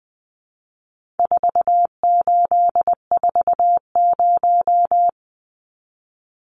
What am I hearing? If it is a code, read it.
Morse code, heard as a keyed tone, about 20 wpm, 700 Hz. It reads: 4840
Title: Numbers - 4-Digit 20wpm